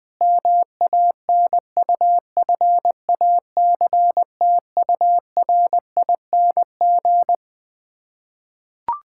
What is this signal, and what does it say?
Morse code, heard as a keyed tone, about 20 wpm, 700 Hz. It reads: MANUFACTURING
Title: K1USN SST QSO Practice 20wpm